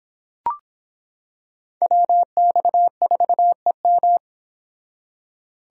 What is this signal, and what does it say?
Morse code, heard as a keyed tone, about 26 wpm, 700 Hz. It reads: WX4EM